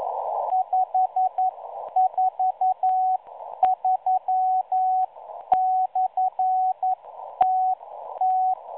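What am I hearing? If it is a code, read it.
Morse code, heard as a keyed tone, about 11 wpm, spaced as for 10 wpm, 750 Hz. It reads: 543/TT
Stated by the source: Morse signal received on 5463.0 kHz via a WebSDR receiver, catalogued as numbers station M03.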